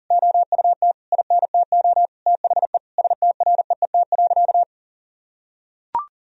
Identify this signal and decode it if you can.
Morse code, heard as a keyed tone, about 40 wpm, 700 Hz. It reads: OUT INTO THE STREET.
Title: Sentences from Top 1000 Words - Review Part 4 - 40wpm